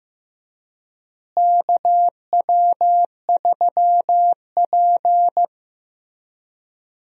Morse code, heard as a keyed tone, about 15 wpm, 700 Hz. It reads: KW3P
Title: USA Call Signs - 4 Characters - Repeated 3x - 15wpm